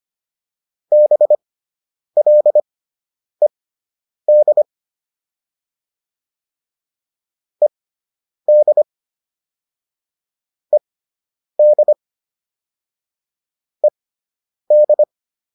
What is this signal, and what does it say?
Morse code, heard as a keyed tone, about 25 wpm, spaced as for 9 wpm, 600 Hz. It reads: BLED ED ED ED